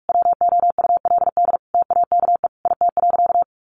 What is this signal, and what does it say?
Morse code, heard as a keyed tone, about 44 wpm, 700 Hz. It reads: WOULD TAKE IT.
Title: Sentences from Top 700 Words 45wpm